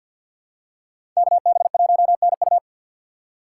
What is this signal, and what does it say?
Morse code, heard as a keyed tone, about 49 wpm, 700 Hz. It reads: KB1NU
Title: USA Call Signs - 5 Characters - 2x Character Spacing - 50wpm